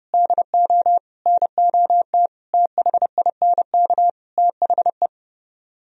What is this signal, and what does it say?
Morse code, heard as a keyed tone, about 32 wpm, 700 Hz. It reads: DO NOT THINK THE